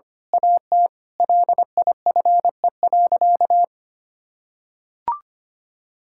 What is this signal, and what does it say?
Morse code, heard as a keyed tone, about 25 wpm, 700 Hz. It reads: AT LIFE.
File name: practice / 25wpm / Difficult Sentences with Hints - 25wpm